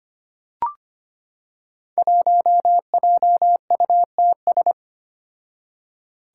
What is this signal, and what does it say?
Morse code, heard as a keyed tone, about 25 wpm, 700 Hz. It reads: 1JUTS
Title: Five Challenging Letter-Number Combinations 25wpm